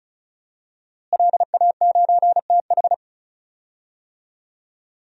Morse code, heard as a keyed tone, about 36 wpm, 700 Hz. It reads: LA9TH
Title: DXCC-100 Call Signs - 5 Characters - Repeated 3x - 35wpm